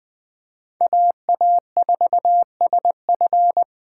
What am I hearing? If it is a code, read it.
Morse code, heard as a keyed tone, about 23 wpm, 700 Hz. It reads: AA4SF
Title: Speed Racing - USA 2x2 Call Signs 20wpm